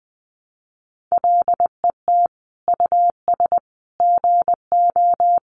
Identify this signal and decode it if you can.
Morse code, heard as a keyed tone, about 20 wpm, 700 Hz. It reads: LET US GO
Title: Sentences from Top 1100 Words 20wpm